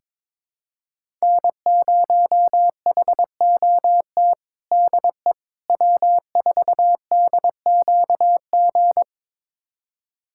Morse code, heard as a keyed tone, about 22 wpm, 700 Hz. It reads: N0HOT DE W4DQG
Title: USA Callsign DE Callsign - 5 Characters - 22wpm